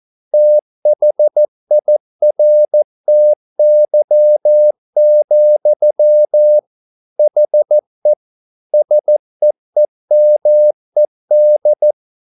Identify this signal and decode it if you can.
Morse code, heard as a keyed tone, about 14 wpm, 600 Hz. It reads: THIRTY, HE SEEMED